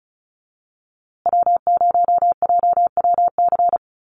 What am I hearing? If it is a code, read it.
Morse code, heard as a keyed tone, about 34 wpm, 700 Hz. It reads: W0JWC
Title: USA Call Signs Repeated 2x 35wpm